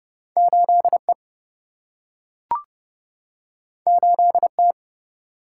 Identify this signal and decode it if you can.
Morse code, heard as a keyed tone, about 30 wpm, 700 Hz. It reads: 8E 8T
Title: Single-Letter Single-Number 30wpm